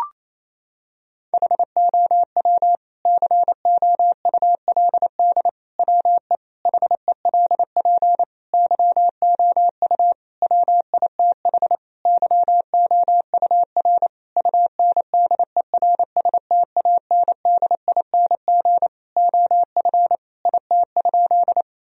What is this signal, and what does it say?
Morse code, heard as a keyed tone, about 28 wpm, 700 Hz. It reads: HOW COULD WE HELP YOU WITH YOUR UNDERSTANDING OF IT?
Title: Sentences from Top 1000 Words - Review Part 3 - 28wpm